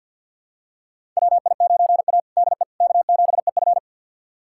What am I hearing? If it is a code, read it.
Morse code, heard as a keyed tone, about 51 wpm, 690 Hz. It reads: WI9A DE K7EF
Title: USA Callsign DE Callsign 50wpm